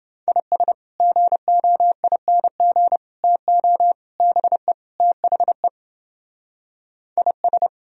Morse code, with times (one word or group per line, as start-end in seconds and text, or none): 0.3-0.7 IS
1.0-3.0 GOING
3.2-3.9 TO
4.2-4.7 BE
5.0-5.7 THE
7.2-7.7 IS